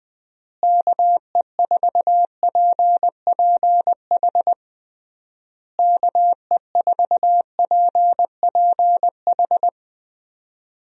Morse code, heard as a keyed tone, about 20 wpm, 700 Hz. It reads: KE4PPH KE4PPH
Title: USA Call Signs - 6 Characters - Repeated 3x - 20wpm